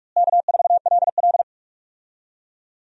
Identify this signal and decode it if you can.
Morse code, heard as a keyed tone, about 46 wpm, 700 Hz. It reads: K4LL
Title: USA Call Signs - 4 Characters - Repeated 3x - 45wpm